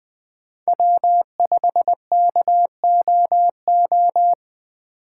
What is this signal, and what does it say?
Morse code, heard as a keyed tone, about 20 wpm, 700 Hz. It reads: W5KOO
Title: USA Call Signs - 5 Characters - 4x Character Spacing - 20wpm